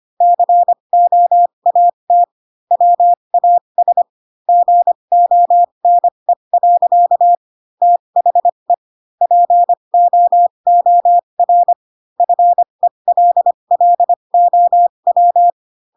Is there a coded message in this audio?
Yes: Morse code from 0.2 to 15.5 s, heard as a keyed tone, about 25 wpm, spaced as for 22 wpm, 700 Hz. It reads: COAT WAS GONE. THE POOR FELLOW